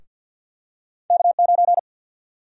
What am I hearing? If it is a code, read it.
Morse code, heard as a keyed tone, about 53 wpm, 700 Hz. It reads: K9
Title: USA Call Sign Prefixes - Repeated 2x - 50wpm